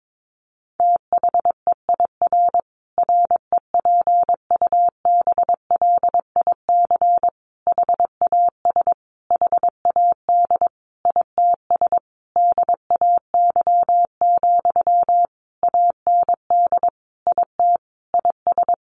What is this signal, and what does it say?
Morse code, heard as a keyed tone, about 22 wpm, 700 Hz. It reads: THEIR REPUBLIC HAS HAD ITS DAY, AND IT IS
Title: Sentences from Top 1500 Words 22wpm